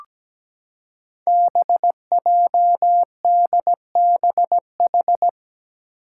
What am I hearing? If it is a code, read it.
Morse code, heard as a keyed tone, about 17 wpm, 700 Hz. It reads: BJDBH
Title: Five Challenging Letter-Number Combinations 17wpm